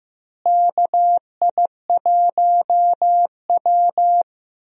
Morse code, heard as a keyed tone, about 15 wpm, 700 Hz. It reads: KI1W
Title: USA Call Signs - 4 Characters - Repeated 3x - 15wpm